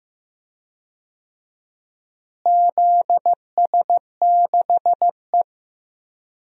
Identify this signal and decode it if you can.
Morse code, heard as a keyed tone, about 15 wpm, 700 Hz. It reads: ZS6E
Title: DXCC-100 Call Signs - 4 Characters - Repeated 2x - 15wpm